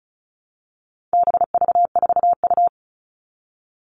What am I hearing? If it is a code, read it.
Morse code, heard as a keyed tone, about 34 wpm, 700 Hz. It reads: BV4U
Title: All Cut Numbers - 4-Digit 35wpm